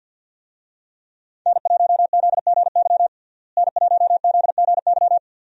Morse code, heard as a keyed tone, about 51 wpm, 700 Hz. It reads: N1ZGY N1ZGY
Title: USA Call Signs - 5 Characters - Repeated 2x - 50wpm